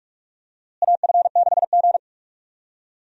Morse code, heard as a keyed tone, about 43 wpm, 700 Hz. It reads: AF6G
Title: USA Call Signs - 4 Characters - Repeated 3x - 45wpm